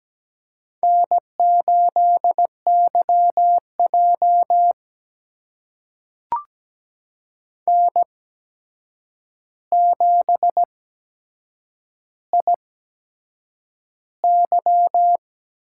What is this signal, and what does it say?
Morse code, heard as a keyed tone, about 17 wpm, spaced as for 7 wpm, 700 Hz. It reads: N8YJ N7IY
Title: USA Call Signs - 4 Characters - 8x Character Spacing - 17wpm